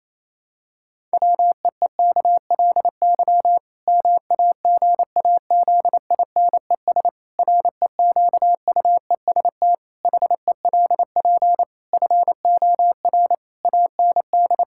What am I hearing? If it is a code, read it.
Morse code, heard as a keyed tone, about 28 wpm, 700 Hz. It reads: WEEKLY MAGAZINES REQUEST HELP FOR AND
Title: Pangram Challenge v2 - 28wpm